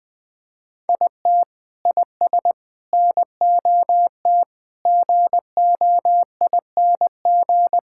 Morse code, heard as a keyed tone, about 20 wpm, 700 Hz. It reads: IT IS NOT GOING
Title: Sentences with 5-Word Phrases - 20wpm